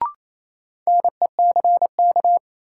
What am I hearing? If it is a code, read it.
Morse code, heard as a keyed tone, about 28 wpm, 700 Hz. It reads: NECK
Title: Sets of 5 Words Encore - New Word Drill - 28wpm